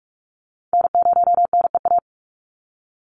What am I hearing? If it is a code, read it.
Morse code, heard as a keyed tone, about 44 wpm, 700 Hz. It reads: N0NEA
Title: USA Call Signs 45wpm